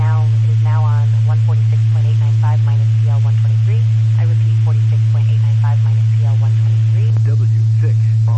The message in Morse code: none

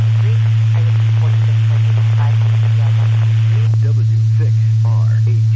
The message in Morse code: none

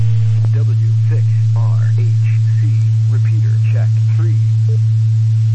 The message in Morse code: none

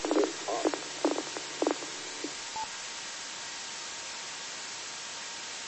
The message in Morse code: none

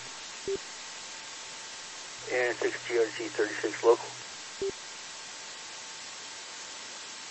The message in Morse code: none

